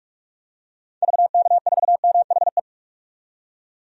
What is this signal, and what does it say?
Morse code, heard as a keyed tone, about 44 wpm, 700 Hz. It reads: VK4MHE
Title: DXCC-100 Call Signs - 6 Characters - 45wpm